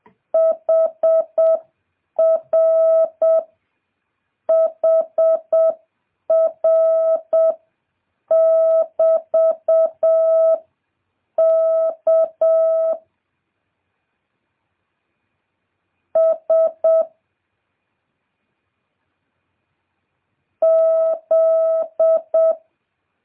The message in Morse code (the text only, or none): HR HR <BT> K S Z